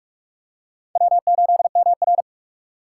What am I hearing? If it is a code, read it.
Morse code, heard as a keyed tone, about 46 wpm, 700 Hz. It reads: W8MR
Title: USA Call Signs - 4 Characters - 4x Character Spacing - 45wpm